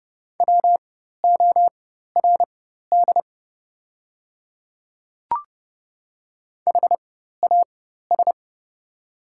Morse code, heard as a keyed tone, about 29 wpm, spaced as for 14 wpm, 700 Hz. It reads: WORD HAS
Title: 4x Character Spacing - Top 300 Words - 30wpm